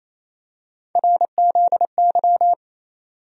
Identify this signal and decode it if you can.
Morse code, heard as a keyed tone, about 27 wpm, 700 Hz. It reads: RZY
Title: USA Call Sign Postfixes - 3 Letters 28wpm